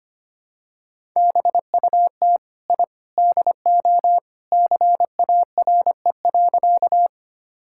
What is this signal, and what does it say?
Morse code, heard as a keyed tone, about 25 wpm, 700 Hz. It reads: BUT I DO CARE.
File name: practice / 25wpm / Sentences with 3-Word Phrases - 25wpm